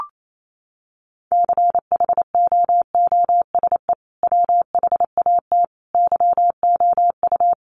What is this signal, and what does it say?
Morse code, heard as a keyed tone, about 28 wpm, 700 Hz. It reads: CHOOSE WHAT YOU
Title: Sentences from Top 700 Words 28wpm